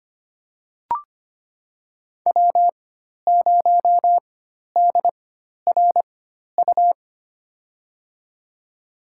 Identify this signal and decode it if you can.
Morse code, heard as a keyed tone, about 25 wpm, spaced as for 12 wpm, 700 Hz. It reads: W0DRU
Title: USA Call Signs - 4x Character Spacing - 25wpm